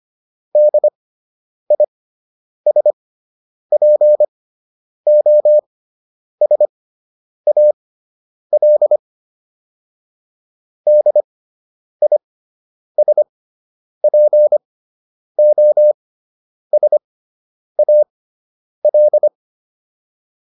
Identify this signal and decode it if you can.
Morse code, heard as a keyed tone, about 25 wpm, spaced as for 9 wpm, 600 Hz. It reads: DISPOSAL DISPOSAL